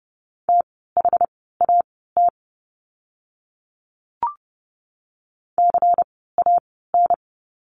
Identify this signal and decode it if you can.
Morse code, heard as a keyed tone, about 30 wpm, spaced as for 17 wpm, 700 Hz. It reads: THAT CAN
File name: course / 30:10wpm / 108 Lesson 18f - Letter m - Words 30wpm